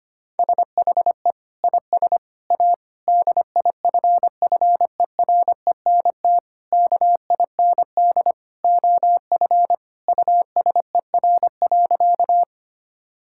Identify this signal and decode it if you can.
Morse code, heard as a keyed tone, about 25 wpm, 700 Hz. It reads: SHE IS A DIFFERENT KIND OF USER.
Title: Sentences from Top 1000 Words - Review Part 5 - 25wpm